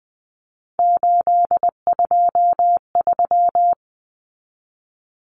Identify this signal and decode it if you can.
Morse code, heard as a keyed tone, about 20 wpm, 700 Hz. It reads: 823
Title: Common Cut Numbers - 3-Digit 20wpm